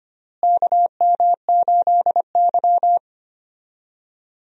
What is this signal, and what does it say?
Morse code, heard as a keyed tone, about 25 wpm, 700 Hz. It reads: KM8Y
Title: USA Call Signs - 4 Characters - Repeated 2x - 25wpm